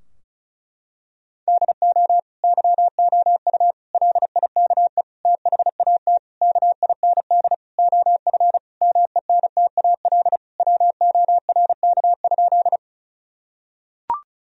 Morse code, heard as a keyed tone, about 36 wpm, 700 Hz. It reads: DO YOU LIKE THAT KIND OF MENTAL WORK?
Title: Sentences from Top 1000 Words - Review Part 4 - 35wpm